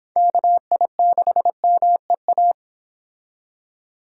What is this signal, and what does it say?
Morse code, heard as a keyed tone, about 29 wpm, 700 Hz. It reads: KI6MEA